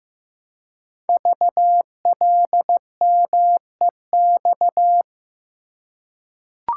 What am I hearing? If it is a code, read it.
Morse code, heard as a keyed tone, about 15 wpm, 700 Hz. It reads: VLMEX
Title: Code Groups - 5 Letters - 15wpm